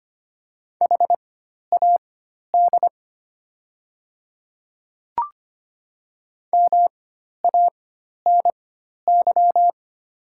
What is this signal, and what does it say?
Morse code, heard as a keyed tone, about 25 wpm, spaced as for 12 wpm, 700 Hz. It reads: HAD MANY
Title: 4x Character Spacing - Top 100 Words - 25wpm